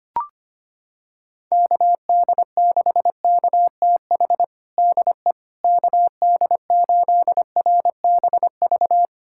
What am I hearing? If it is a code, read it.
Morse code, heard as a keyed tone, about 25 wpm, 700 Hz. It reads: KD6KTH DE KD8RBV
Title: USA Callsign DE Callsign - 6 Characters - 25wpm